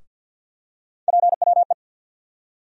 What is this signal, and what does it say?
Morse code, heard as a keyed tone, about 50 wpm, 700 Hz. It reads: PWE
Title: USA Call Sign Postfixes - 3 Letters 50wpm